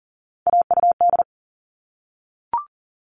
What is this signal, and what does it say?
Morse code, heard as a keyed tone, about 40 wpm, 700 Hz. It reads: AUD
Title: All Cut Numbers - 3-Digit 40wpm